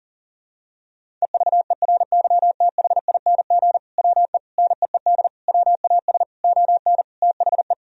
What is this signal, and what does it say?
Morse code, heard as a keyed tone, about 40 wpm, 700 Hz. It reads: EVERYTHING WE NEED WAS ON THE
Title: Sentences from Top 1000 Words - Review Part 1 - 40wpm